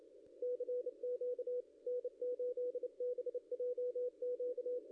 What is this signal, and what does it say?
Morse code, heard as a keyed tone, about 31 wpm, 490 Hz, with radio noise. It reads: CQ N8BJQ
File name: contests / ss-cw-k5tr-op-n5rz -0700